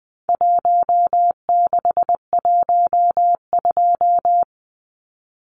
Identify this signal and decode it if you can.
Morse code, heard as a keyed tone, about 20 wpm, 700 Hz. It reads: 1612
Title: Numbers - Contest Style - 4-Digit 20wpm